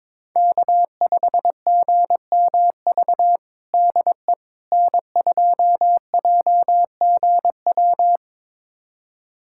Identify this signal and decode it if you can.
Morse code, heard as a keyed tone, about 21 wpm, 700 Hz. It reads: K5GMV DE N2JGW
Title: USA Callsign DE Callsign - 5 Characters - 22wpm